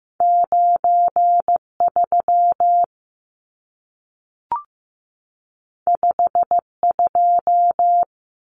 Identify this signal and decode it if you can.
Morse code, heard as a keyed tone, about 15 wpm, 700 Hz. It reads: 9 3 5 2
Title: Numbers - 2-Digit 15wpm